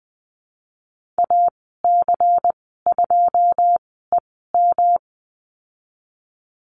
Morse code, heard as a keyed tone, about 20 wpm, spaced as for 15 wpm, 700 Hz. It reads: AC2EM
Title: 167 Lesson 27g - Number 7 - Call Signs 20wpm